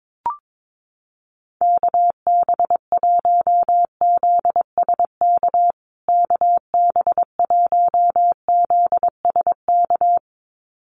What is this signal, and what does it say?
Morse code, heard as a keyed tone, about 22 wpm, 700 Hz. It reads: KB1ZSK KB1ZSK
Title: USA Call Signs Repeated 2x 22wpm